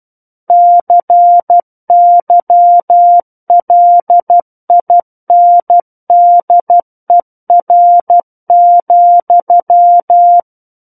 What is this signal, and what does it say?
Morse code, heard as a keyed tone, about 12 wpm, 700 Hz. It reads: CYLINDER,
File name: Book One / 03-Chapter 3 - 12 WPM